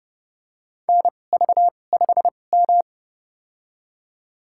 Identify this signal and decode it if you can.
Morse code, heard as a keyed tone, about 29 wpm, spaced as for 21 wpm, 700 Hz. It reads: NV5M
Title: USA Call Signs - 4 Characters - 2x Character Spacing - 30wpm